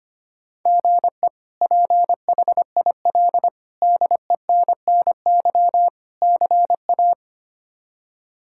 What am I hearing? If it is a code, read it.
Morse code, heard as a keyed tone, about 25 wpm, 700 Hz. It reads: GE PHIL DENNY CA
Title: K1USN SST QSO Practice - 25wpm